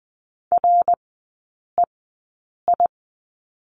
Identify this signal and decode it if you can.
Morse code, heard as a keyed tone, about 20 wpm, spaced as for 8 wpm, 700 Hz. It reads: REI
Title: 053 Lesson 10e - Letter r - Three Characters 20wpm